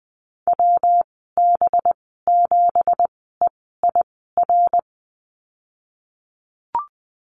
Morse code, heard as a keyed tone, about 19 wpm, spaced as for 14 wpm, 700 Hz. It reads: WB7EIR
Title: 193 Lesson 31g - Letter v - Call Signs 20wpm